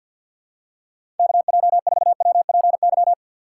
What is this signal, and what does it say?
Morse code, heard as a keyed tone, about 52 wpm, 700 Hz. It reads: KJ4WPY
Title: USA Call Signs - 6 Characters - 3x Character Spacing - 50wpm